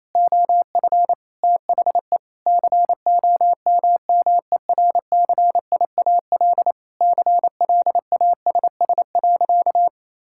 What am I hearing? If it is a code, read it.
Morse code, heard as a keyed tone, about 27 wpm, 700 Hz. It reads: OF THE COMMERCIAL CLASS.
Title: Sentences from Top 1000 Words - Review Part 5 - 28wpm